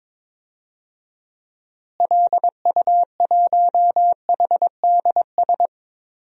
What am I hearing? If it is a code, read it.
Morse code, heard as a keyed tone, about 22 wpm, 700 Hz. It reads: LU1HDS
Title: DXCC-100 Call Signs - 6 Characters - Repeated 3x - 22wpm